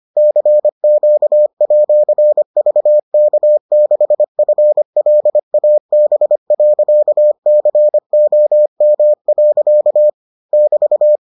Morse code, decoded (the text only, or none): CQ@VK6FLAB.COM. <BT>